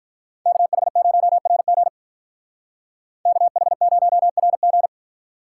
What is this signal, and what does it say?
Morse code, heard as a keyed tone, about 45 wpm, 700 Hz. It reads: KH0RG KH0RG